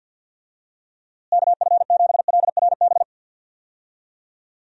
Morse code, heard as a keyed tone, about 50 wpm, 700 Hz. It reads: KF7LRB
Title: USA Call Signs - 6 Characters - Repeated 2x - 50wpm